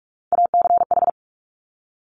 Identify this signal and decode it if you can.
Morse code, heard as a keyed tone, about 46 wpm, 700 Hz. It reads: ACH
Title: Top Three Letter Combinations in Words 45wpm